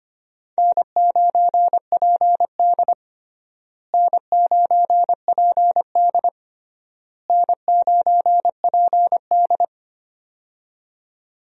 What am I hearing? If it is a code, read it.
Morse code, heard as a keyed tone, about 25 wpm, 700 Hz. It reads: N9PD N9PD N9PD
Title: USA Call Signs - 4 Characters - Repeated 3x - 25wpm